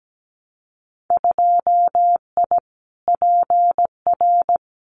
Morse code, heard as a keyed tone, about 17 wpm, 700 Hz. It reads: 2I PR
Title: Winter Field Day Exchanges Encore 17wpm